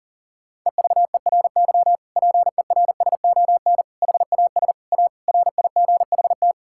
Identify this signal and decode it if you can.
Morse code, heard as a keyed tone, about 40 wpm, 700 Hz. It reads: EVERY PERSON HAS A RIGHT
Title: Sentences from Top 1000 Words - Review Part 5 - 40wpm